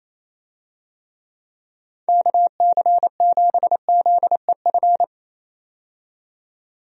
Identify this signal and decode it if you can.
Morse code, heard as a keyed tone, about 28 wpm, 700 Hz. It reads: KC7ZEF
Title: USA Call Signs - 6 Characters - 28wpm